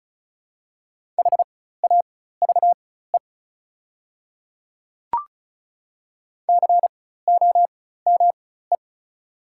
Morse code, heard as a keyed tone, about 36 wpm, spaced as for 16 wpm, 700 Hz. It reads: HAVE COME
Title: 4x Character Spacing - Top 100 Words - 35wpm